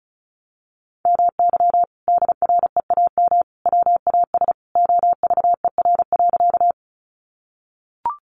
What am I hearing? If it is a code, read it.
Morse code, heard as a keyed tone, about 34 wpm, 700 Hz. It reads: MY DREAM WAS OVER.
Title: Sentences from Top 1500 Words 35wpm